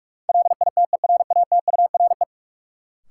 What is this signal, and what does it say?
Morse code, heard as a keyed tone, about 45 wpm, 700 Hz. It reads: LITERATURE
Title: Sentences from Top 600 Words - New Word Drill - 45wpm